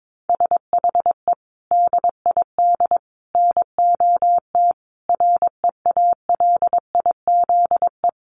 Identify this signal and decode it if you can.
Morse code, heard as a keyed tone, about 22 wpm, 700 Hz. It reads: SHE DID NOT REALIZE
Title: Sentences from Top 1000 Words 22wpm